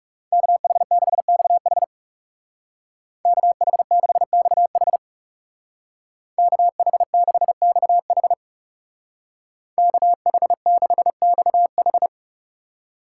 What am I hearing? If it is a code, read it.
Morse code, heard as a keyed tone, about 38 wpm, 700 Hz. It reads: KH6XH KH6XH KH6XH KH6XH